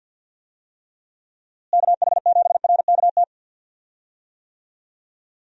Morse code, heard as a keyed tone, about 48 wpm, 700 Hz. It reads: KH7RKT
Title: USA Call Signs - 6 Characters - Repeated 3x - 50wpm